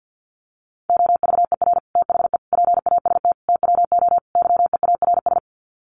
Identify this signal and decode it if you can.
Morse code, heard as a keyed tone, about 48 wpm, 700 Hz. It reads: OVER THE PAST TWO YEARS
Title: Sets of 5 Words Encore 50wpm